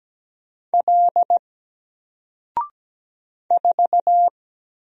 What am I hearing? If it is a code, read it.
Morse code, heard as a keyed tone, about 17 wpm, 700 Hz. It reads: L 4
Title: Single Letter-Number Alt 17wpm